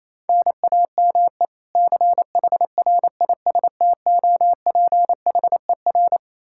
Morse code, heard as a keyed tone, about 28 wpm, 700 Hz. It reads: NAME CHRISTOPHER
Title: Practice QSO - Exchange 1 - Beginning - 28wpm